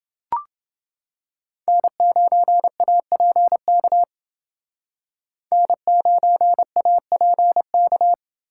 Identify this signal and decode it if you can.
Morse code, heard as a keyed tone, about 29 wpm, 700 Hz. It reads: N9APK N9APK